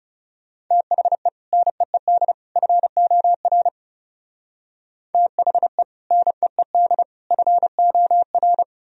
Morse code, heard as a keyed tone, about 34 wpm, 700 Hz. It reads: THE NEED FOR THE NEED FOR
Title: Speed-Racing - Sets of 3 Words 30wpm